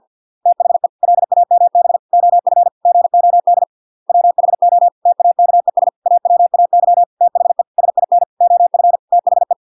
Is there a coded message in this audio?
Yes: Morse code from 0.5 to 9.5 s, heard as a keyed tone, about 50 wpm, 700 Hz. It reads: THE LAMB OF GOD WHO TAKES AWAY THE SIN OF THE